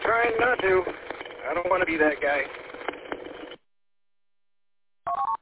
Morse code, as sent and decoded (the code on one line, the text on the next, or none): none